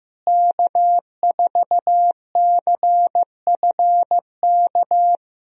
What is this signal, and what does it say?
Morse code, heard as a keyed tone, about 16 wpm, 700 Hz. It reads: K4CFK
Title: USA Call Signs - 5 Characters - Repeated 3x - 15wpm